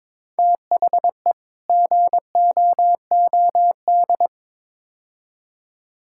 Morse code, heard as a keyed tone, about 22 wpm, 700 Hz. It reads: THE GOOD
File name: practice / 22wpm / Build A Phrase - 5-Word Phrases - 22wpm